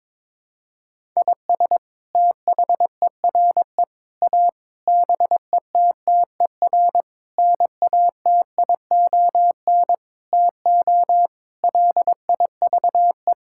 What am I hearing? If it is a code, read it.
Morse code, heard as a keyed tone, about 22 wpm, 700 Hz. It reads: IS THERE A BETTER NATION TO LIVE
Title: Sentences from Top 1000 Words - Review Part 6 - 22wpm